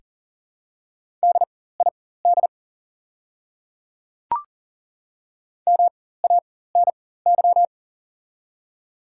Morse code, heard as a keyed tone, about 40 wpm, spaced as for 19 wpm, 700 Hz. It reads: DID MANY
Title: 4x Character Spacing - Top 200 Words - 40wpm